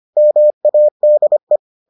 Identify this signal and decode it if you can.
Morse code, heard as a keyed tone, about 25 wpm, 600 Hz. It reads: MADE